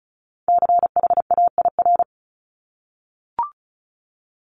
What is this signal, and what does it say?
Morse code, heard as a keyed tone, about 36 wpm, 700 Hz. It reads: CHAIR
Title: Multi-letter Phonemes Repeated 3x 35wpm